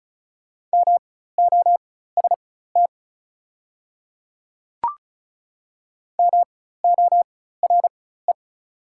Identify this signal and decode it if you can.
Morse code, heard as a keyed tone, about 36 wpm, spaced as for 16 wpm, 690 Hz. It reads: MOST MORE
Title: 4x Character Spacing - Top 300 Words - 35wpm